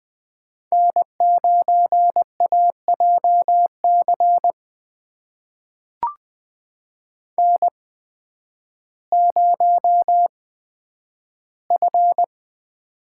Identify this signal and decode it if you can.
Morse code, heard as a keyed tone, about 20 wpm, spaced as for 11 wpm, 700 Hz. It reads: N9AJC N0F
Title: USA Call Signs - 5 Characters - 8x Character Spacing - 20wpm